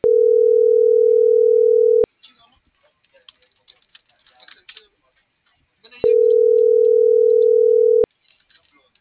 Ringback tone (North America), 440 + 480 Hz.